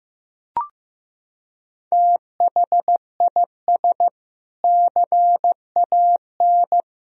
Morse code, heard as a keyed tone, about 15 wpm, 700 Hz. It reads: THIS CAN
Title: Sentences from Top 1000 Words - Review Part 3 - 15wpm